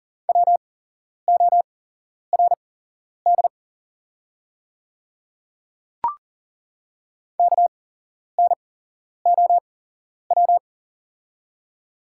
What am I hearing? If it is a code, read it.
Morse code, heard as a keyed tone, about 40 wpm, spaced as for 11 wpm, 690 Hz. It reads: WORD KNOW
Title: 8x Character Spacing - Top 500 Words - 40wpm